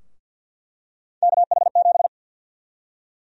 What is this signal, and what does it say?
Morse code, heard as a keyed tone, about 46 wpm, 700 Hz. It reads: KH7
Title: USA Call Sign Prefixes - 50wpm